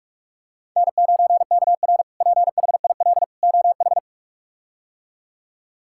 Morse code, heard as a keyed tone, about 46 wpm, 700 Hz. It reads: N9KR PHIL OH
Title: K1USN SST QSO Practice - 45wpm